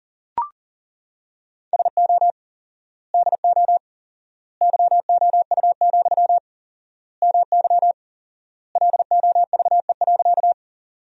Morse code, heard as a keyed tone, about 40 wpm, 700 Hz. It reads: SO DO YOU, MY LOVE.
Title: 4x Word-Spacing - Sentences from Top 400 Words - 40wpm